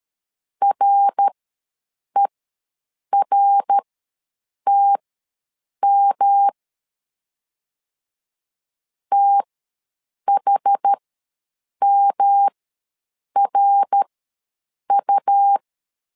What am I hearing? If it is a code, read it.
Morse code, heard as a keyed tone, about 12 wpm, spaced as for 7 wpm, 800 Hz. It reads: RERTM THMRU
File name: LEZIONI / 3 MWRO / 3 MWR ws 12wpm 1809